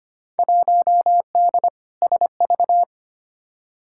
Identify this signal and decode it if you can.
Morse code, heard as a keyed tone, about 25 wpm, 700 Hz. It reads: 1D SV
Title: Field Day Exchanges - 25wpm